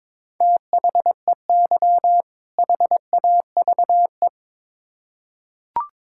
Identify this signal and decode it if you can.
Morse code, heard as a keyed tone, about 22 wpm, 700 Hz. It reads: THEY HAVE
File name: Build A Phrase - 2-Word Phrases - 22wpm